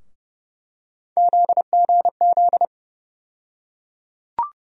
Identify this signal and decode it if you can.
Morse code, heard as a keyed tone, about 30 wpm, 700 Hz. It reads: ZGZ
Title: USA Call Sign Postfixes - 3 Letters 30wpm